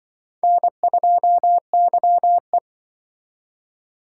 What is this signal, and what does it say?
Morse code, heard as a keyed tone, about 26 wpm, 700 Hz. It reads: N2YE